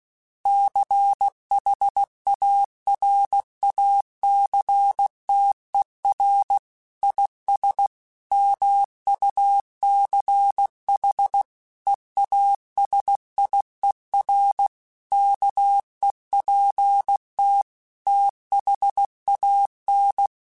Morse code, heard as a keyed tone, about 16 wpm, 800 Hz. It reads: CHARACTER IS MUCH EASIER KEPT THAN
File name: Quotes / Quotes-mp3s / Character @15